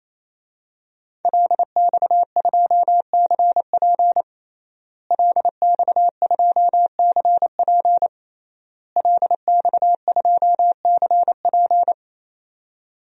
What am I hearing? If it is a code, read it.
Morse code, heard as a keyed tone, about 27 wpm, 700 Hz. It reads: LX2CP LX2CP LX2CP
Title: DXCC-100 Call Signs - 5 Characters - Repeated 3x - 28wpm